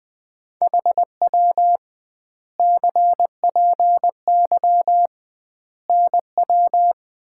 Morse code, heard as a keyed tone, about 20 wpm, 700 Hz. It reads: HW CPY NW
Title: Common QSO Phrases - 2x Word-Spacing - 20wpm